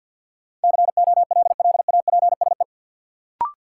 Morse code, heard as a keyed tone, about 52 wpm, 700 Hz. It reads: COLLAPSE
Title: Sets of 5 Words Encore - New Word Drill - 50wpm